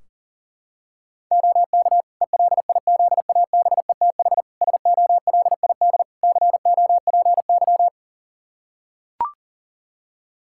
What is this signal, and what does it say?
Morse code, heard as a keyed tone, about 40 wpm, 700 Hz. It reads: OK ELIZABETH SOLID COPY
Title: Practice QSO - Exchange 3 - More Info - 40wpm